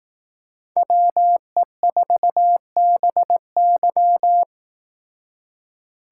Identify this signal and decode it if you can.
Morse code, heard as a keyed tone, about 18 wpm, 700 Hz. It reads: WE4BY
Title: Speed Racing - USA 2x2 Call Signs 15wpm